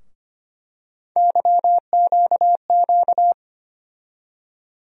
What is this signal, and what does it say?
Morse code, heard as a keyed tone, about 25 wpm, 700 Hz. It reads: YQQ
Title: USA Call Sign Postfixes - 3 Letters 25wpm